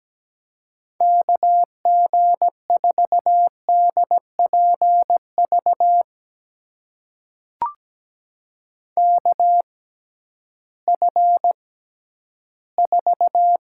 Morse code, heard as a keyed tone, about 17 wpm, spaced as for 11 wpm, 700 Hz. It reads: KG4DPV KF4
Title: USA Call Signs - 6 Characters - 6x Character Spacing - 17wpm